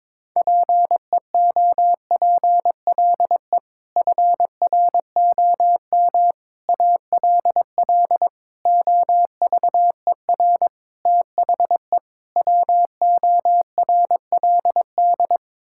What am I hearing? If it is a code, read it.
Morse code, heard as a keyed tone, about 23 wpm, 700 Hz. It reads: PEOPLE FROM ALL OVER THE WORLD
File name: practice / 22wpm / Sets of 6 Words - 22wpm